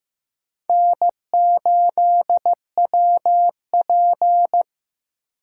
Morse code, heard as a keyed tone, about 18 wpm, 700 Hz. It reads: N8WP
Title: Speed Racing - USA 1x2 Call Signs 15wpm